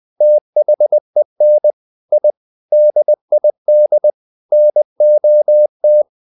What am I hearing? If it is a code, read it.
Morse code, heard as a keyed tone, about 20 wpm, 600 Hz. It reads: THEN I DID NOT
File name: the old man and the sea part 2 20wpm